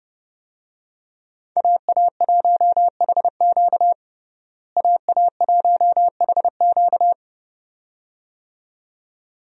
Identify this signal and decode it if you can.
Morse code, heard as a keyed tone, about 32 wpm, 700 Hz. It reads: AA1HQ AA1HQ
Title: USA Call Signs - 5 Characters - Repeated 2x - 30wpm